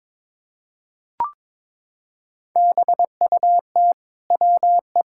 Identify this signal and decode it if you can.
Morse code, heard as a keyed tone, about 22 wpm, 700 Hz. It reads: BUT WE
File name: Sets of 9 Words - 22wpm